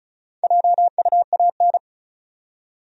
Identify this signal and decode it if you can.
Morse code, heard as a keyed tone, about 36 wpm, 700 Hz. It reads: JUAN
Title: USA Common Names - Max 5 Letters - 35wpm